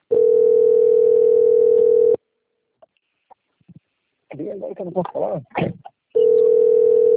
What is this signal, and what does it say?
Ringback tone (North America), 440 + 480 Hz, with speech.
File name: out- - - - - .6410